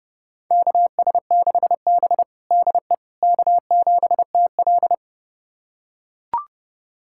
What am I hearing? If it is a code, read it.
Morse code, heard as a keyed tone, about 30 wpm, 700 Hz. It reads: KS6B DE K7TL
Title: USA Callsign DE Callsign - 4 Characters - 30wpm